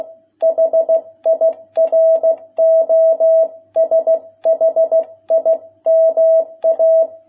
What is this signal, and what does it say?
Morse code, heard as a keyed tone, about 16 wpm, 650 Hz. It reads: HIROSHIMA